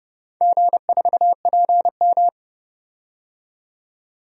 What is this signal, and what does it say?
Morse code, heard as a keyed tone, about 30 wpm, 700 Hz. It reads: G4PM
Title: DXCC-100 Call Signs - 4 Characters - 30wpm